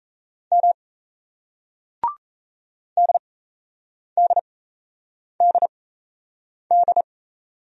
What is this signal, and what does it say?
Morse code, heard as a keyed tone, about 38 wpm, spaced as for 8 wpm, 700 Hz. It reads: M DDDD